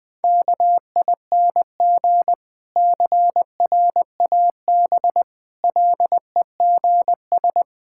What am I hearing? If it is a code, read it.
Morse code, heard as a keyed tone, about 20 wpm, 700 Hz. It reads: KING CRAB LEGS
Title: Difficult Words with Hints - 20wpm